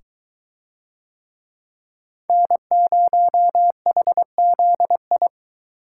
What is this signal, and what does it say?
Morse code, heard as a keyed tone, about 22 wpm, 700 Hz. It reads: N0HZI